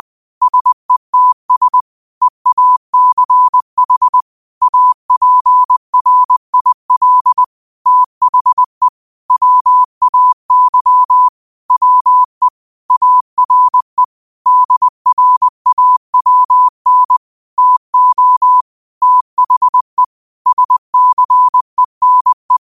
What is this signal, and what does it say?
Morse code, heard as a keyed tone, about 20 wpm, 1000 Hz. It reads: SETS EACH APRIL THE WAY WE ARE DRAWN TO THE SCENE
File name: qotdfiles / 20wpm